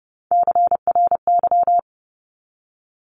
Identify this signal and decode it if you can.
Morse code, heard as a keyed tone, about 30 wpm, 700 Hz. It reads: CRY
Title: Top 500 Words 30wpm